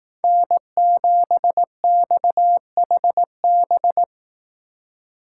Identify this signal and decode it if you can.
Morse code, heard as a keyed tone, about 19 wpm, 700 Hz. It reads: N7XHB
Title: Speed Racing - USA Call Signs 15wpm